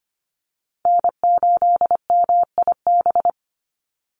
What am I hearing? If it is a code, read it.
Morse code, heard as a keyed tone, about 25 wpm, 700 Hz. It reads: N8MIB
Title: USA Call Signs 25wpm